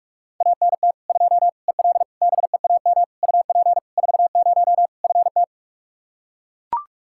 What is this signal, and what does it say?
Morse code, heard as a keyed tone, about 41 wpm, 700 Hz. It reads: ANT 2 EL BEAM UP 40 FT
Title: K1USN SST QSO Practice 45wpm